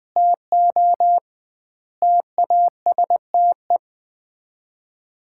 Morse code, heard as a keyed tone, about 20 wpm, 700 Hz. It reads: TO TASTE
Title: 2x Word-Spacing - Sets of 5 Words - 20wpm